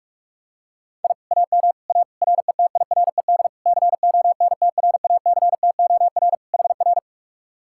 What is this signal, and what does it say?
Morse code, heard as a keyed tone, about 48 wpm, 700 Hz. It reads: I AM A RETIRED CONTRACTOR HR